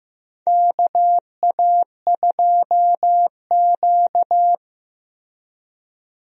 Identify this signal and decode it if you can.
Morse code, heard as a keyed tone, about 15 wpm, 700 Hz. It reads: KA2Q
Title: USA Call Signs - 4 Characters - Repeated 2x - 15wpm